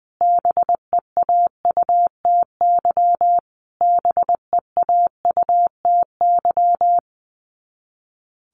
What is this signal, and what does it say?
Morse code, heard as a keyed tone, about 20 wpm, 700 Hz. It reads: BEAUTY BEAUTY
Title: Top 500 Words Repeated 20wpm